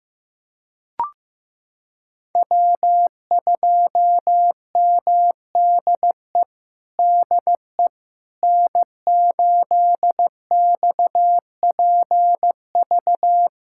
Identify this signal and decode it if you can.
Morse code, heard as a keyed tone, about 15 wpm, 700 Hz. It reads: W2MDE DE N8XPV
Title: USA Callsign DE Callsign - 5 Characters - 15wpm